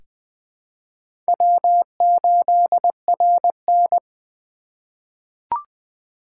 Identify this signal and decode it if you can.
Morse code, heard as a keyed tone, about 20 wpm, 700 Hz. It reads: W8RN
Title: USA Call Signs - 4 Characters - 4x Character Spacing - 20wpm